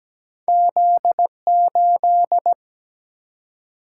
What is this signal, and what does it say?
Morse code, heard as a keyed tone, about 17 wpm, 700 Hz. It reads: Z 8
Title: Single-Letter Single-Number 17wpm